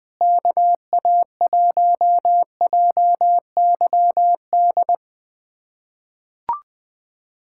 Morse code, heard as a keyed tone, about 21 wpm, 700 Hz. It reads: KA1JYD
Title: USA Callsign DE Callsign 20wpm